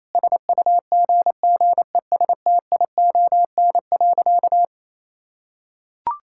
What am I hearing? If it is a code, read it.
Morse code, heard as a keyed tone, about 27 wpm, 700 Hz. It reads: SUGGESTION.
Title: Sentences from Top 1600 Words 28wpm